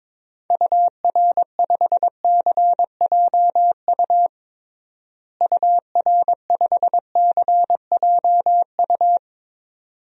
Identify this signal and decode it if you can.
Morse code, heard as a keyed tone, about 21 wpm, 700 Hz. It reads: UR5CJU UR5CJU
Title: DXCC-100 Call Signs - 6 Characters - Repeated 2x - 22wpm